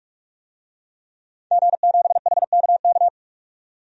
Morse code, heard as a keyed tone, about 41 wpm, 700 Hz. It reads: G7HKK
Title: DXCC-100 Call Signs - 5 Characters - 45wpm